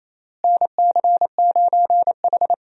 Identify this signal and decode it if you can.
Morse code, heard as a keyed tone, about 28 wpm, 700 Hz. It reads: NC9H
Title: USA Call Signs - 4 Characters - Repeated 3x - 28wpm